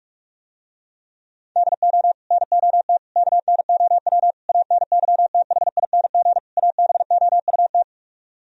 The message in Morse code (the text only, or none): DO NOT KNOW ANYTHING ABOUT